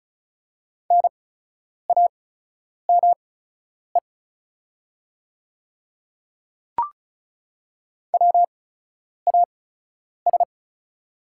Morse code, heard as a keyed tone, about 36 wpm, spaced as for 10 wpm, 700 Hz. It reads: NAME WAS
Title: 8x Character Spacing - Top 200 Words - 35wpm